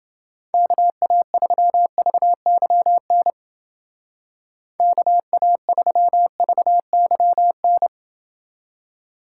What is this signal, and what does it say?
Morse code, heard as a keyed tone, about 28 wpm, 700 Hz. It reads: KA3VYN KA3VYN